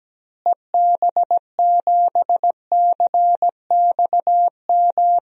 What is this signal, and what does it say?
Morse code, heard as a keyed tone, about 17 wpm, 700 Hz. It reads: EB7CXM
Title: DXCC-100 Call Signs - 6 Characters - Repeated 2x - 17wpm